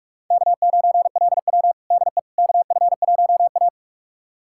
Morse code, heard as a keyed tone, about 46 wpm, 700 Hz. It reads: K9LW DE KF1A
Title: USA Callsign DE Callsign - 4 Characters - 45wpm